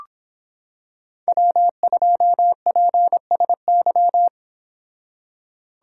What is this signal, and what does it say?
Morse code, heard as a keyed tone, about 25 wpm, 700 Hz. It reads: W2PSY